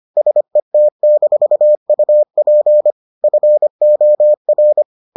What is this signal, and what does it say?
Morse code, heard as a keyed tone, about 25 wpm, 600 Hz. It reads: SET-UP FOR